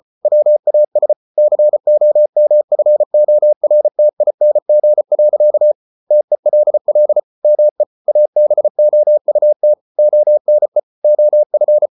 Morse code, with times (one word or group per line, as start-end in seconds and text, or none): 0.2-1.1 WAS
1.4-5.7 COMFORTING.
6.1-7.2 TELL
7.4-7.8 ME
8.1-9.7 ABOUT
10.0-10.8 ONE
11.0-11.9 OF